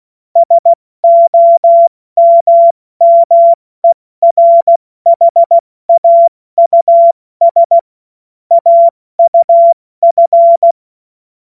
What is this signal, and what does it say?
Morse code, heard as a keyed tone, about 16 wpm, 680 Hz. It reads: SOMMERHAUS AUF
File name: Morsekurs / Witze / Witz 07. Februar 2023 70 F80